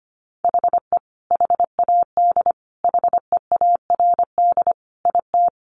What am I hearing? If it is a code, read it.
Morse code, heard as a keyed tone, about 25 wpm, 700 Hz. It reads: HE HAD HEARD IT
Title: Sentences from Top 1400 Words 25wpm